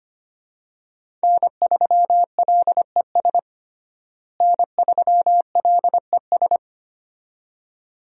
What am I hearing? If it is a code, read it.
Morse code, heard as a keyed tone, about 25 wpm, 700 Hz. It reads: N3LES N3LES
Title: USA Call Signs - 5 Characters - Repeated 2x - 25wpm